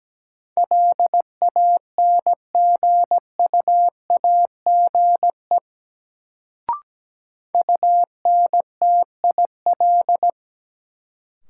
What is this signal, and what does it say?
Morse code, heard as a keyed tone, about 17 wpm, 700 Hz. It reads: LANGUAGE UNTIL
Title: Sentences from Top 500 Words - New Word Drill - 17wpm